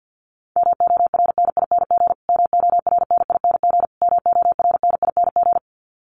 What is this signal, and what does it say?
Morse code, heard as a keyed tone, about 50 wpm, 700 Hz. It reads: MORNING MORNING MORNING
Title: Top 500 Words Repeated 50wpm